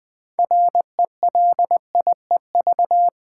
Morse code, heard as a keyed tone, about 20 wpm, 700 Hz. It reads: RELIEV
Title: Build A Word - 3 Letters At A Time - 8-Letter Words - 20wpm